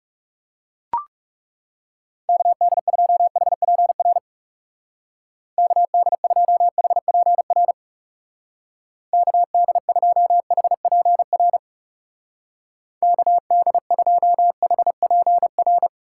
Morse code, heard as a keyed tone, about 37 wpm, 700 Hz. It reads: KD2HPR KD2HPR KD2HPR KD2HPR
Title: Speed Racing - USA 2x3 Call Signs 30wpm